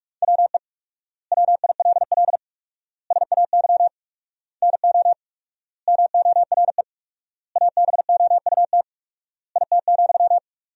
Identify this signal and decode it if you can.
Morse code, heard as a keyed tone, about 42 wpm, 700 Hz. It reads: WE WILL SAY NO MORE ABOUT IT,
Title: 4x Word-Spacing - Sentences from Top 200 Words - 45wpm